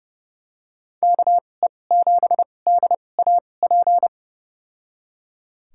Morse code, heard as a keyed tone, about 30 wpm, spaced as for 22 wpm, 700 Hz. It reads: KE7DAP
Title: USA Call Signs - 2x Character Spacing - 30wpm